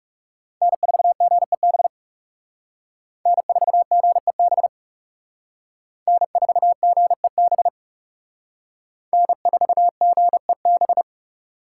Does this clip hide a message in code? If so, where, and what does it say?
Yes: Morse code from 0.6 to 11.0 s, heard as a keyed tone, about 39 wpm, 700 Hz. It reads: N4GEB N4GEB N4GEB N4GEB